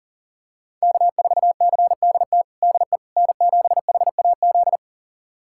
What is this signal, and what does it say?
Morse code, heard as a keyed tone, about 37 wpm, 700 Hz. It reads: K4CDT DE N7HAZ
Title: USA Callsign DE Callsign - 5 Characters - 40wpm